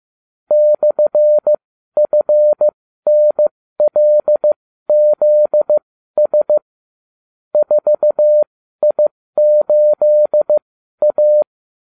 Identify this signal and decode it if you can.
Morse code, heard as a keyed tone, about 15 wpm, spaced as for 12 wpm, 600 Hz. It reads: /FNLZS 4I8A